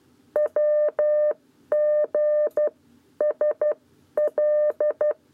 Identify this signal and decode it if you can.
Morse code, heard as a keyed tone, about 12 wpm, spaced as for 10 wpm, 600 Hz. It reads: WG SL
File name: 7b lezione cw 1